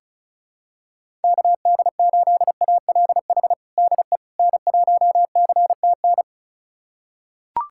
Morse code, heard as a keyed tone, about 34 wpm, 700 Hz. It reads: KD8ALH DE N1CTN
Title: USA Callsign DE Callsign 35wpm